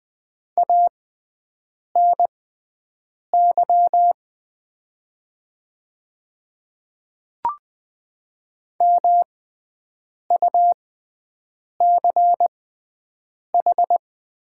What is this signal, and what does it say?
Morse code, heard as a keyed tone, about 20 wpm, spaced as for 7 wpm, 700 Hz. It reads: ANY MUCH
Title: 6x Character Spacing - Top 300 Words - 20wpm